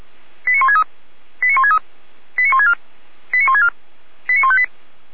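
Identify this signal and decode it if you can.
Five-tone selective call (ZVEI), heard as quick runs of five tones, tones of about 70 ms. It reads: 89163 89163 89164 89164 89158
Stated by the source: Received via SDR; ZVEI five-tone call 89150.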